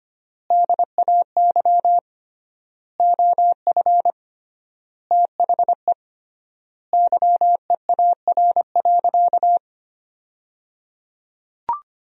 Morse code, heard as a keyed tone, about 25 wpm, 700 Hz. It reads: DAY OF THE YEAR.
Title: 3x Word-Spacing - Sentences from Top 300 Words - 25wpm